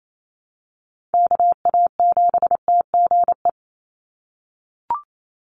Morse code, heard as a keyed tone, about 28 wpm, 700 Hz. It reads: KA7TGE
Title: USA Call Signs 28wpm